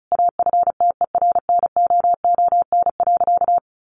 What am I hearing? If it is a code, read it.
Morse code, heard as a keyed tone, about 34 wpm, 700 Hz. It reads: AFTERNOON.
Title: Sentences from Top 1500 Words 35wpm